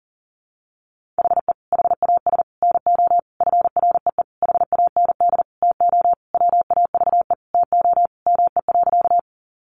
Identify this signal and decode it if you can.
Morse code, heard as a keyed tone, about 37 wpm, 700 Hz. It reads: HE HAS NO FREE HAND TO WAVE TO ME.
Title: Sentences from Top 1400 Words 40wpm